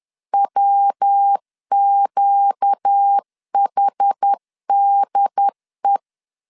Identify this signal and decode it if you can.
Morse code, heard as a keyed tone, about 10 wpm, 800 Hz. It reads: WQHDE
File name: CW / DET / 1 6 1500C S B 10wpm 4137